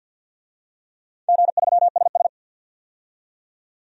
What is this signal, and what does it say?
Morse code, heard as a keyed tone, about 47 wpm, 700 Hz. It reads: G3SS
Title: DXCC-100 Call Signs - 4 Characters - 50wpm